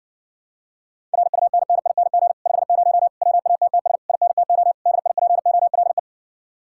Morse code, heard as a keyed tone, about 60 wpm, 700 Hz. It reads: RUNNING 50 WATTS INTO DIPOLE